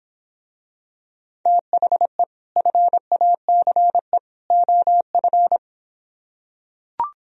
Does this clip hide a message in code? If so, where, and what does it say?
Yes: Morse code from 1.5 to 5.6 s, heard as a keyed tone, about 26 wpm, 700 Hz. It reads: THE FACE OF